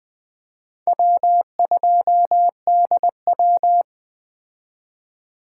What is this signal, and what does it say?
Morse code, heard as a keyed tone, about 20 wpm, 700 Hz. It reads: W2DW
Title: Speed Racing - USA 1x2 Call Signs 17wpm